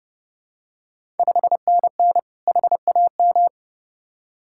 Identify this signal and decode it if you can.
Morse code, heard as a keyed tone, about 30 wpm, 690 Hz. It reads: 5NN HAM
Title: Ontario QSO Party Exchange 30wpm